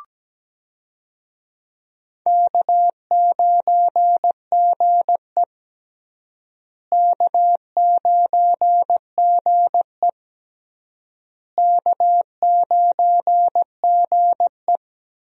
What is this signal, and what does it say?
Morse code, heard as a keyed tone, about 17 wpm, 700 Hz. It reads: K9GE K9GE K9GE
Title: USA Call Signs - 4 Characters - Repeated 3x - 17wpm